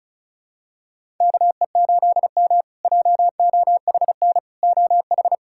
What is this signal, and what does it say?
Morse code, heard as a keyed tone, about 34 wpm, 700 Hz. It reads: KE8M JOHN OH
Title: K1USN SST QSO Practice - 35wpm